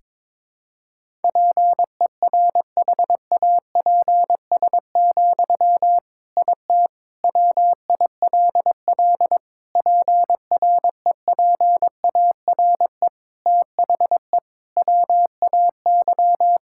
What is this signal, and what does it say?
Morse code, heard as a keyed tone, about 22 wpm, 700 Hz. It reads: PERHAPS, IT WILL PREPARE THE WAY
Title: Sentences from Top 1000 Words - Review Part 6 - 22wpm